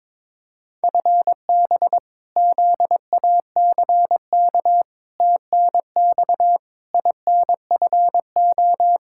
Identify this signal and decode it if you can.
Morse code, heard as a keyed tone, about 22 wpm, 700 Hz. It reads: FB ZACK TNX INFO
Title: Practice QSO - Exchange 3 - More Info - 22wpm